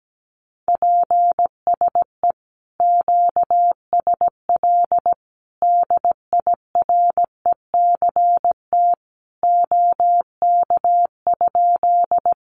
Morse code, heard as a keyed tone, about 17 wpm, 700 Hz. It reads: PSE QSL DIRECT OK?
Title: Common QSO Phrases 17wpm